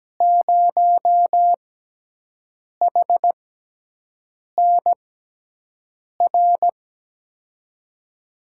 Morse code, heard as a keyed tone, about 17 wpm, spaced as for 6 wpm, 700 Hz. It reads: 0HNR
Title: USA Call Signs - 6x Character Spacing - 17wpm